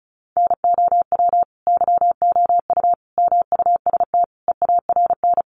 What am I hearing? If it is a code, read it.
Morse code, heard as a keyed tone, about 34 wpm, 700 Hz. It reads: NOW YOU MUST EARN
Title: Sentences from Top 1500 Words 35wpm